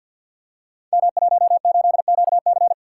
Morse code, heard as a keyed tone, about 52 wpm, 700 Hz. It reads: M18QC
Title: Code Groups - 5 Letter:Numbers - 50wpm